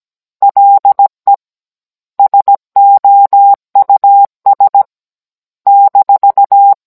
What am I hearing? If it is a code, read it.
Morse code, heard as a keyed tone, about 17 wpm, spaced as for 14 wpm, 800 Hz. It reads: LE SOUS-